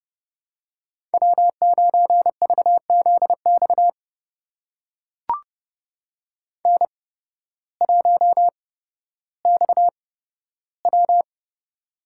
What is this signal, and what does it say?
Morse code, heard as a keyed tone, about 31 wpm, spaced as for 14 wpm, 700 Hz. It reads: W9VZX N1XW
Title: USA Call Signs - 8x Character Spacing - 30wpm